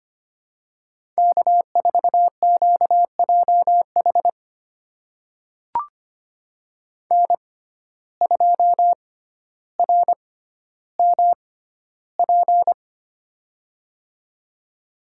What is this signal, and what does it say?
Morse code, heard as a keyed tone, about 25 wpm, spaced as for 13 wpm, 700 Hz. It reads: K4QJH N2RMP
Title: USA Call Signs - 5 Characters - 6x Character Spacing - 25wpm